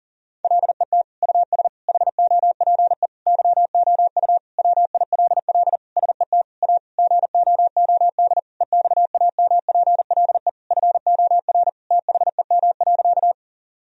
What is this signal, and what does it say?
Morse code, heard as a keyed tone, about 40 wpm, 700 Hz. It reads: LET US HOPE YOU WILL SET A GOOD EXAMPLE FOR THEM.
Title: Sentences from Top 1000 Words - Review Part 1 - 40wpm